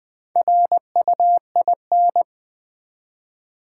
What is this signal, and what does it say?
Morse code, heard as a keyed tone, about 20 wpm, 700 Hz. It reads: RUIN
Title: Four Letter Words - 20wpm